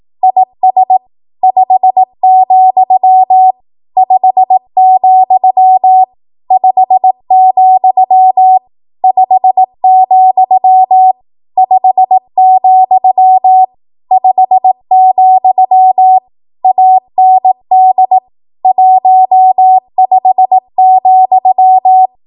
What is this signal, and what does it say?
Morse code, heard as a keyed tone, about 18 wpm, 750 Hz. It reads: IS 5, 5, 5, 5, 5, 5, AND 15,
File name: Morse / 18WPM